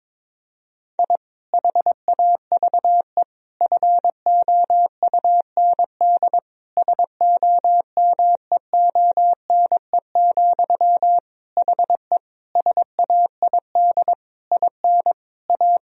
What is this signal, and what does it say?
Morse code, heard as a keyed tone, about 22 wpm, 700 Hz. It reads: I HAVE FOUND SOMEONE, HE SAID IN A
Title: Sentences from Top 1000 Words - Review Part 1 - 22wpm